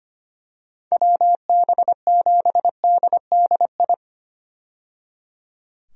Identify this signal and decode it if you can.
Morse code, heard as a keyed tone, about 25 wpm, 700 Hz. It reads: WB7DDI
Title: USA Call Signs - 6 Characters - 25wpm